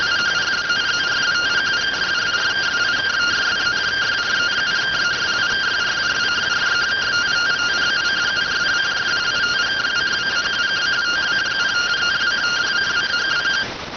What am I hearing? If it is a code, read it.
Two-tone teleprinter signal, about 170 Hz shift.